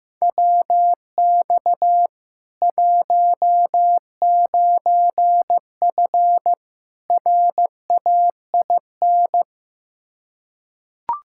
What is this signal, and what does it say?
Morse code, heard as a keyed tone, about 15 wpm, 700 Hz. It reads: WX 19F RAIN
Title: K1USN SST QSO Practice 15wpm